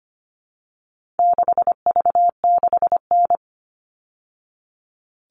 Morse code, heard as a keyed tone, about 25 wpm, 700 Hz. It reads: 6V6N
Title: All Cut Numbers - 4-Digit 25wpm